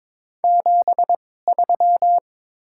Morse code, heard as a keyed tone, about 22 wpm, spaced as for 16 wpm, 690 Hz. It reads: 73
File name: practice / 22wpm / Top 20 QSO Elements - 2x Character Spacing - 22wpm